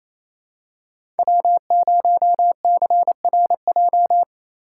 Morse code, heard as a keyed tone, about 28 wpm, 700 Hz. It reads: W0CRJ
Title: USA Call Signs - 5 Characters - 4x Character Spacing - 28wpm